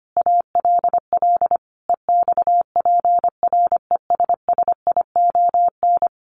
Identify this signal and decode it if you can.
Morse code, heard as a keyed tone, about 25 wpm, 700 Hz. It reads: ALL EXPRESSION
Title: A Sound Of Thunder 25wpm